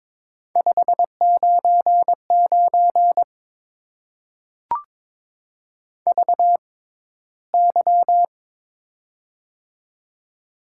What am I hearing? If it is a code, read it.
Morse code, heard as a keyed tone, about 22 wpm, spaced as for 14 wpm, 700 Hz. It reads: 599 VY
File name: Top 90 QSO Elements - 6x Character Spacing - 22wpm